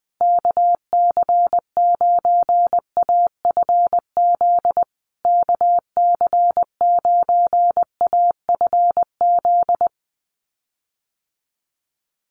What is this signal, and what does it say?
Morse code, heard as a keyed tone, about 20 wpm, 700 Hz. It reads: KC9AFZ KC9AFZ
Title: USA Call Signs Repeated 3x 20wpm